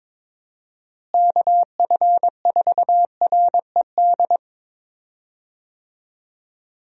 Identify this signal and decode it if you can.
Morse code, heard as a keyed tone, about 21 wpm, 700 Hz. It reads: KF4RED
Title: USA Call Signs - 6 Characters - Repeated 3x - 22wpm